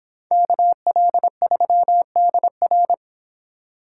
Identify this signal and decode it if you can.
Morse code, heard as a keyed tone, about 26 wpm, 700 Hz. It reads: KL3DR